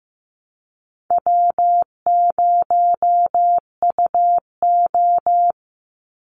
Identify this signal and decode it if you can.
Morse code, heard as a keyed tone, about 15 wpm, 700 Hz. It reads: W0UO
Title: Call Signs Repeated 2x - 15wpm